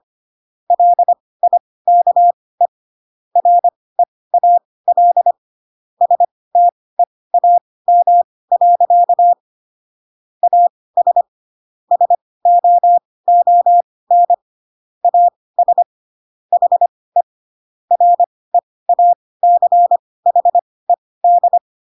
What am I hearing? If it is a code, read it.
Morse code, heard as a keyed tone, about 25 wpm, spaced as for 18 wpm, 700 Hz. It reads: LIKE REAL STEAM. AS SOON AS HE REACHED
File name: Pinocchio ch3 12 25 ik1hge